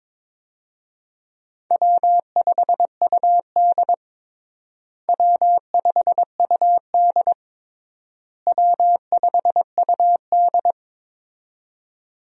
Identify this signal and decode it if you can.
Morse code, heard as a keyed tone, about 22 wpm, 700 Hz. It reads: W5UD W5UD W5UD
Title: USA Call Signs - 4 Characters - Repeated 3x - 22wpm